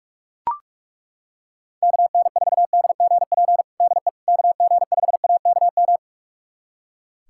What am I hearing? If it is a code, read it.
Morse code, heard as a keyed tone, about 41 wpm, 700 Hz. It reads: KN4DGP DE KG5AKM
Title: USA Callsign DE Callsign - 6 Characters - 45wpm